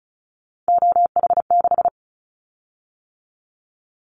Morse code, heard as a keyed tone, about 35 wpm, 700 Hz. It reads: OH6
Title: Callsign Prefixes for Top 25 DXCC - 35wpm